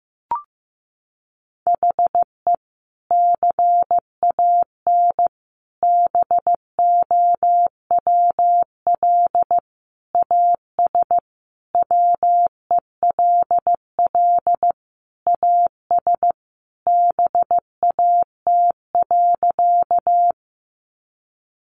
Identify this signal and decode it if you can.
Morse code, heard as a keyed tone, about 15 wpm, 700 Hz. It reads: HE CAN BOWL AS WELL AS BAT.
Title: Sentences from Top 1500 Words 15wpm